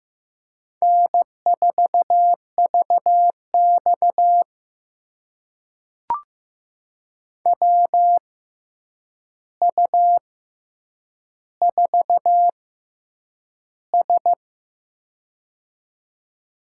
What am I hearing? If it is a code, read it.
Morse code, heard as a keyed tone, about 15 wpm, spaced as for 8 wpm, 700 Hz. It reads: N4VX WU4S
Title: USA Call Signs - 4 Characters - 6x Character Spacing - 15wpm